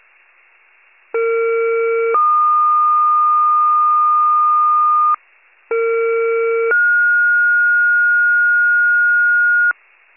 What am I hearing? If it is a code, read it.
Two-tone page, heard as a one-second tone, then a three-second tone, about 460 then 1220 Hz, about 460 then 1430 Hz. It is heard from 1.1 to 9.7 s.
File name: pager / 23 FIRE / wcalert